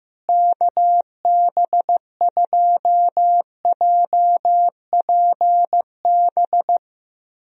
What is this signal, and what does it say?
Morse code, heard as a keyed tone, about 14 wpm, 700 Hz. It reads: KB2JPB
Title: USA Call Signs - 6 Characters - Repeated 3x - 15wpm